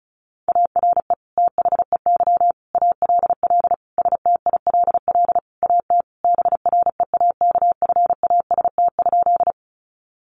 Morse code, heard as a keyed tone, about 34 wpm, 700 Hz. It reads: ARE THEY ALL STILL AT BREAKFAST?
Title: Sentences from Top 1500 Words 35wpm